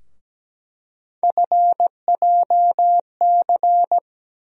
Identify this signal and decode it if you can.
Morse code, heard as a keyed tone, about 17 wpm, 700 Hz. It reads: FJC
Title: USA Call Sign Postfixes - 3 Letters 17wpm